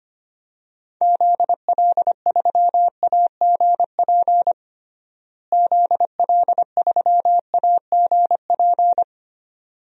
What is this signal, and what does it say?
Morse code, heard as a keyed tone, about 25 wpm, 700 Hz. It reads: ZL3AGP ZL3AGP
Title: DXCC-100 Call Signs - 6 Characters - Repeated 2x - 25wpm